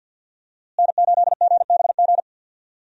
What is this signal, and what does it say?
Morse code, heard as a keyed tone, about 50 wpm, 700 Hz. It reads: N8GBG
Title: USA Call Signs - 5 Characters - Repeated 3x - 50wpm